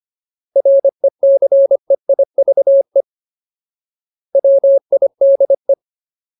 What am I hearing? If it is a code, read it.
Morse code, heard as a keyed tone, about 25 wpm, 550 Hz. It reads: RECEIVE WIDE
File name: Ultimate Word List / 25WPM Ultimate